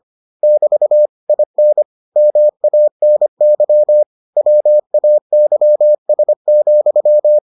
Morse code, heard as a keyed tone, about 25 wpm, 600 Hz. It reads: <BT> IN MANY WAYS,